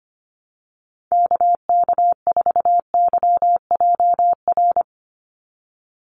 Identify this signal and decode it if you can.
Morse code, heard as a keyed tone, about 25 wpm, 700 Hz. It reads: KK4YJR
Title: USA Call Signs 25wpm